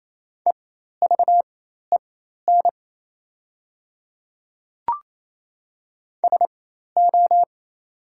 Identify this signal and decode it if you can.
Morse code, heard as a keyed tone, about 28 wpm, spaced as for 13 wpm, 700 Hz. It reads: EVEN SO
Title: 4x Character Spacing - Top 300 Words - 28wpm